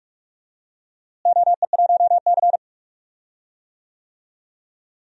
Morse code, heard as a keyed tone, about 46 wpm, 700 Hz. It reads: OE1C
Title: DXCC-100 Call Signs - 4 Characters - Repeated 2x - 45wpm